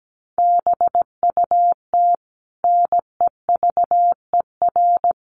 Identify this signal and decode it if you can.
Morse code, heard as a keyed tone, about 17 wpm, 700 Hz. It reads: BUT NEVER
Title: Sentences from Top 500 Words 17wpm